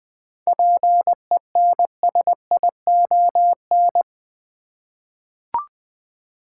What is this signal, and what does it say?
Morse code, heard as a keyed tone, about 20 wpm, 700 Hz. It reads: PENSION
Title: Seven Letter Words - 20wpm